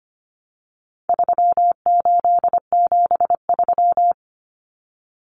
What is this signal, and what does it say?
Morse code, heard as a keyed tone, about 25 wpm, 700 Hz. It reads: 3873
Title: Numbers - 4-Digit 25wpm